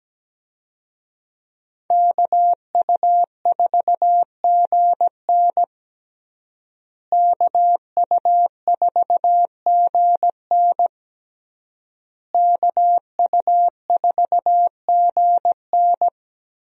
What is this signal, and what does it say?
Morse code, heard as a keyed tone, about 17 wpm, 700 Hz. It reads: KU4GN KU4GN KU4GN
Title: USA Call Signs - 5 Characters - Repeated 3x - 17wpm